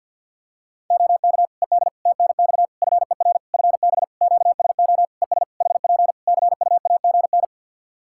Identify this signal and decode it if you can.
Morse code, heard as a keyed tone, about 47 wpm, 700 Hz. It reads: OK ED TNX FER FB QSO ES HP CUAGN
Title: Practice QSO - Exchange 4 - Ending - 50wpm